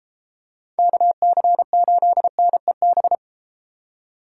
Morse code, heard as a keyed tone, about 32 wpm, 700 Hz. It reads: KC8NEB